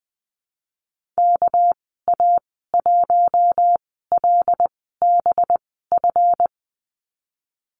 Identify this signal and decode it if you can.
Morse code, heard as a keyed tone, about 20 wpm, spaced as for 14 wpm, 700 Hz. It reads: KA1LBF
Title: 207 Lesson 33g - Letter j - Call Signs 20wpm